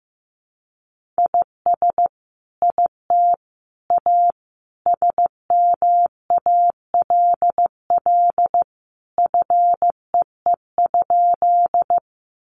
Morse code, heard as a keyed tone, about 15 wpm, 700 Hz. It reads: IS IT A SMALL FEE?
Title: Sentences from Top 1500 Words 15wpm